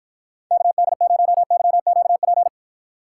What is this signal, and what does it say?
Morse code, heard as a keyed tone, about 52 wpm, 700 Hz. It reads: KD0YQP